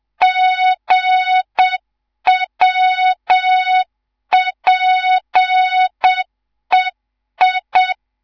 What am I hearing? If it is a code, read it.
Morse code, heard as a keyed tone, about 7 wpm, 3700 Hz, with noise. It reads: GWPEI
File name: Morze-HA2MN / hang / 11 GYAKORLAT